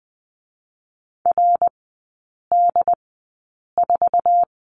Morse code, heard as a keyed tone, about 20 wpm, spaced as for 8 wpm, 700 Hz. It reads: RD4
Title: 087 Lesson 15e - Number 5 - Three Characters 20wpm